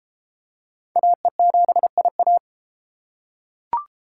Morse code, heard as a keyed tone, about 29 wpm, 700 Hz. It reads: AE7IA